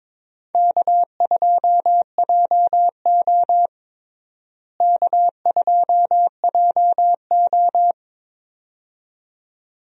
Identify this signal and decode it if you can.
Morse code, heard as a keyed tone, about 21 wpm, 700 Hz. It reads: K2JO K2JO
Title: USA Call Signs - 4 Characters - Repeated 2x - 22wpm